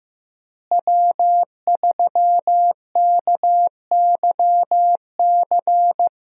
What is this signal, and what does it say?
Morse code, heard as a keyed tone, about 15 wpm, 700 Hz. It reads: W3KYC
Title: USA Call Signs - 5 Characters - Repeated 2x - 15wpm